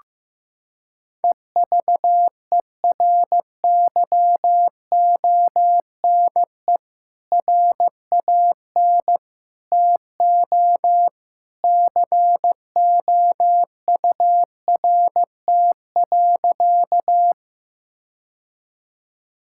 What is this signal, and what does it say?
Morse code, heard as a keyed tone, about 16 wpm, 700 Hz. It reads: EVERYONE RAN TO COURT.
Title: Sentences from Top 1600 Words 15wpm